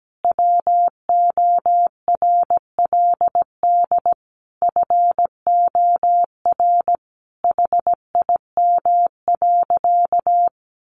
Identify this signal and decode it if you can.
Morse code, heard as a keyed tone, about 17 wpm, 700 Hz. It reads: WORLD FOR HIM.
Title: Sentences from Top 1300 Words 17wpm